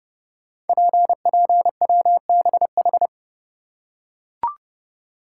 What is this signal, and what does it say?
Morse code, heard as a keyed tone, about 30 wpm, 700 Hz. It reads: PPWBH
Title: Code Groups - 5 Letters - 30wpm